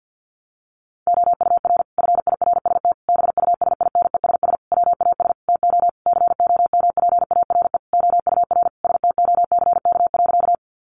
Morse code, heard as a keyed tone, about 50 wpm, 700 Hz. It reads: OUR FIRST BUSINESS WAS TO COMPARE OUR STOCK.
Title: Sentences from Top 1300 Words 50wpm